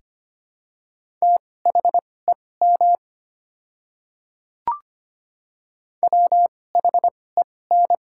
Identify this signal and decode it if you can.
Morse code, heard as a keyed tone, about 25 wpm, spaced as for 18 wpm, 700 Hz. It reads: THEM WHEN
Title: 2x Character Spacing - Top 100 Words - 25wpm